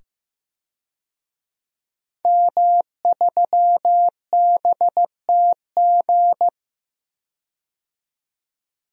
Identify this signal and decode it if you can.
Morse code, heard as a keyed tone, about 15 wpm, 700 Hz. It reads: M3BTG
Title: DXCC-100 Call Signs - 5 Characters - 15wpm